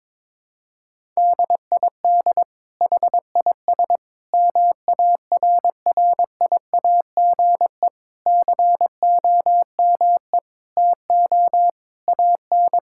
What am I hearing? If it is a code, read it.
Morse code, heard as a keyed tone, about 22 wpm, 700 Hz. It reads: DID HIS MARRIAGE COME TO AN
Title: Sentences from Top 1000 Words - Review Part 5 - 22wpm